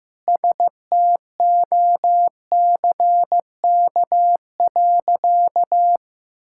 Morse code, heard as a keyed tone, about 16 wpm, 700 Hz. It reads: STOCK.
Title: Sentences from Top 1000 Words - Review Part 5 - 15wpm